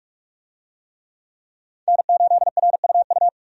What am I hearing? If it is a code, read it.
Morse code, heard as a keyed tone, about 45 wpm, 700 Hz. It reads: N8RUU
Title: USA Call Signs - 5 Characters - 45wpm